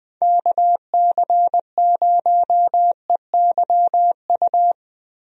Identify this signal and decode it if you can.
Morse code, heard as a keyed tone, about 20 wpm, 700 Hz. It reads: KC0EYU
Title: USA Call Signs - 6 Characters - Repeated 3x - 20wpm